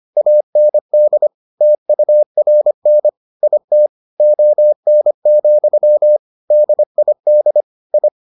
Morse code, heard as a keyed tone, about 25 wpm, 600 Hz. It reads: AND TURN IT ON, DID I